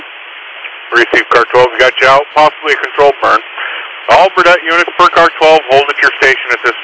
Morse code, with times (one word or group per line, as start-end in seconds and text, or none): none